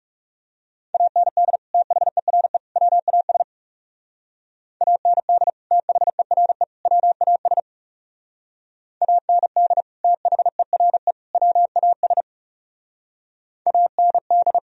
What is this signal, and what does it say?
Morse code, heard as a keyed tone, about 40 wpm, 700 Hz. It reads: AND THERE WAS AND THERE WAS AND THERE WAS AND